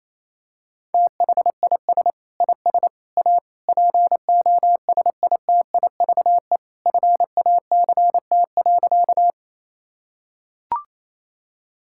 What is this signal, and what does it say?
Morse code, heard as a keyed tone, about 28 wpm, 700 Hz. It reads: THIS IS A POSITIVE FACT.
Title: Sentences from Top 1000 Words - Review Part 5 - 28wpm